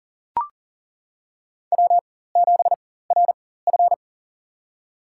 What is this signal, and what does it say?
Morse code, heard as a keyed tone, about 40 wpm, spaced as for 19 wpm, 700 Hz. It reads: W7RF
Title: USA Call Signs - 4 Characters - 4x Character Spacing - 40wpm